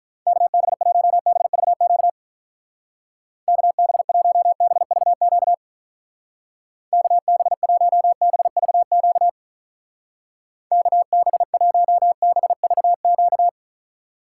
Morse code, heard as a keyed tone, about 43 wpm, 700 Hz. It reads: KB1BVQ KB1BVQ KB1BVQ KB1BVQ
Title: Speed Racing - USA 2x3 Call Signs 35wpm